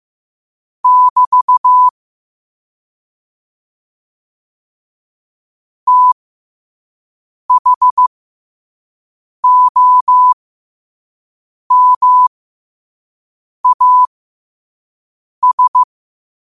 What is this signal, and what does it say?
Morse code, heard as a keyed tone, about 15 wpm, spaced as for 5 wpm, 1000 Hz. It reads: <BT> THOMAS
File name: 5wpm